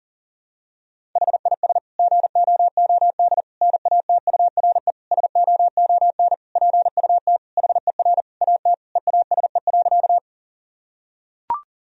Morse code, heard as a keyed tone, about 40 wpm, 700 Hz. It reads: HIS GOOD NATURE SOON PUT HER AT EASE.
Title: Sentences from Top 1000 Words - Review Part 1 - 40wpm